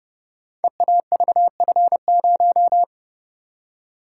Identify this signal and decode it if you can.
Morse code, heard as a keyed tone, about 28 wpm, 700 Hz. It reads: EAVF0
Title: Code Groups - 5 Letter:Numbers - 30wpm